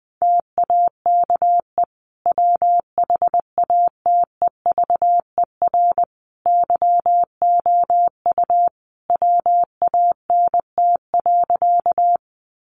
Morse code, heard as a keyed tone, about 20 wpm, 700 Hz. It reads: TAKE WHATEVER YOU WANT.
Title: Sentences from Top 1100 Words 20wpm